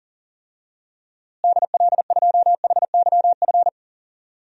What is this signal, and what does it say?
Morse code, heard as a keyed tone, about 40 wpm, 700 Hz. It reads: DL2HYF
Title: DXCC-100 Call Signs - 6 Characters - Repeated 2x - 40wpm